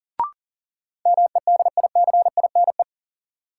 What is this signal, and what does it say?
Morse code, heard as a keyed tone, about 39 wpm, 700 Hz. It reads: MEDICINE
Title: Sentences from Top 900 Words - New Word Drill - 40wpm